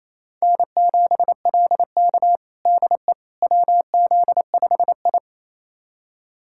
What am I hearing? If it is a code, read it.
Morse code, heard as a keyed tone, about 28 wpm, 700 Hz. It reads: N7LK DE WZ5I
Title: USA Callsign DE Callsign - 4 Characters - 28wpm